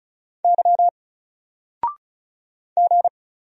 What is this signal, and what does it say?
Morse code, heard as a keyed tone, about 33 wpm, 700 Hz. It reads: Y G
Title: Single Letter-Number Alt 35wpm